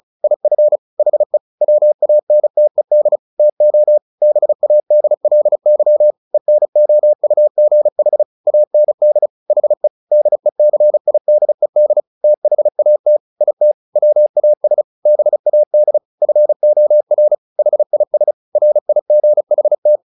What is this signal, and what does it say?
Morse code, heard as a keyed tone, about 34 wpm, 600 Hz. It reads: IF HE WANTED TO BADLY ENOUGH AND HE DECIDED THAT IT WAS BAD FOR HIS RIGHT
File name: the old man and the sea 2 35wpm